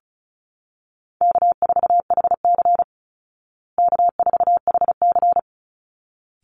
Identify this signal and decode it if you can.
Morse code, heard as a keyed tone, about 34 wpm, 700 Hz. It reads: K4HC K4HC
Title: Easy Call Signs Repeated 2x - 35wpm